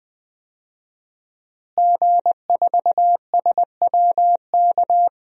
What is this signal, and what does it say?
Morse code, heard as a keyed tone, about 20 wpm, 700 Hz. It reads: G4SWK
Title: DXCC-100 Call Signs - 5 Characters - Repeated 3x - 20wpm